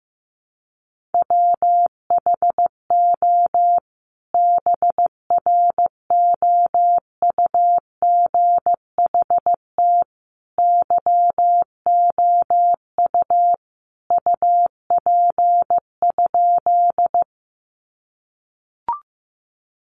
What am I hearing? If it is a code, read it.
Morse code, heard as a keyed tone, about 15 wpm, 700 Hz. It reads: WHO BROUGHT YOU UP?
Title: Sentences from Top 500 Words 15wpm